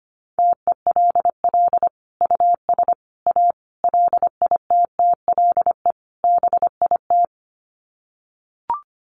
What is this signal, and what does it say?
Morse code, heard as a keyed tone, about 25 wpm, 700 Hz. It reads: TELL US A LITTLE BIT
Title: Sets of 5 Words Encore2 25wpm